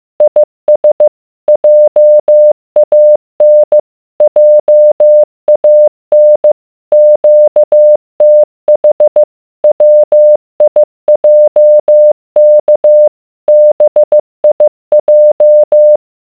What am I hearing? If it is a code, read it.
Morse code, heard as a keyed tone, about 15 wpm, 600 Hz. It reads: IS JAN JAN QTH WIJK BIJ